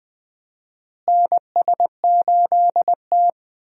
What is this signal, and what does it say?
Morse code, heard as a keyed tone, about 20 wpm, 700 Hz. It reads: NS8T
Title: USA Call Signs - 4 Characters - Repeated 2x - 20wpm